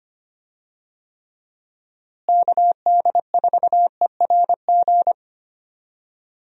Morse code, heard as a keyed tone, about 25 wpm, 700 Hz. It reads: KD4ERG
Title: USA Call Signs - 6 Characters - Repeated 2x - 25wpm